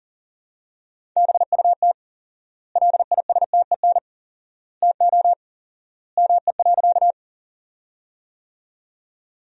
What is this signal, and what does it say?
Morse code, heard as a keyed tone, about 40 wpm, 700 Hz. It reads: BUT LISTEN TO ME.
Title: 4x Word-Spacing - Sentences from Top 400 Words - 40wpm